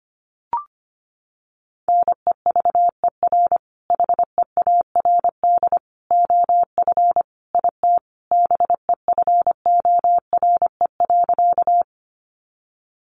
Morse code, heard as a keyed tone, about 25 wpm, 700 Hz. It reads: NEVER HEARD OF IT BEFORE.
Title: Sentences from Top 400 Words 25wpm